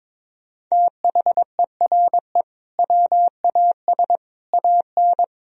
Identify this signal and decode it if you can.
Morse code, heard as a keyed tone, about 22 wpm, 700 Hz. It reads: THERE WAS AN
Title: Sentences from Top 1000 Words - Review Part 4 - 22wpm